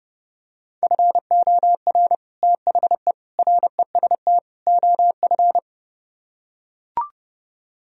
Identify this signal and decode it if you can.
Morse code, heard as a keyed tone, about 30 wpm, 700 Hz. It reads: FOR THE REST OF